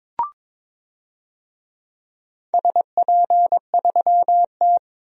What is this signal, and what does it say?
Morse code, heard as a keyed tone, about 22 wpm, 700 Hz. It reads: SP3T
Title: DXCC-100 Call Signs - 4 Characters - Repeated 3x - 22wpm